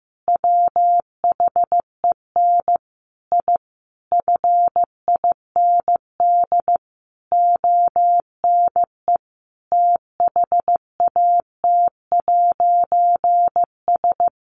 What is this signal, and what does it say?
Morse code, heard as a keyed tone, about 15 wpm, 700 Hz. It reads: WHEN I FIND ONE THAT'S
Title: A Sound Of Thunder 15wpm